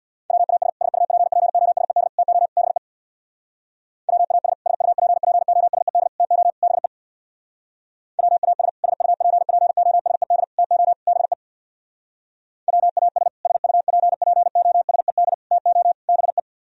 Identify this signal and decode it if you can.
Morse code, heard as a keyed tone, about 64 wpm, 700 Hz. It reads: WAS SUPPOSED TO BE WAS SUPPOSED TO BE WAS SUPPOSED TO BE WAS SUPPOSED TO BE